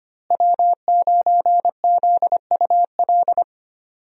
Morse code, heard as a keyed tone, about 25 wpm, 700 Hz. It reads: W9ZUL
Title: USA Call Signs - 5 Characters - Repeated 3x - 25wpm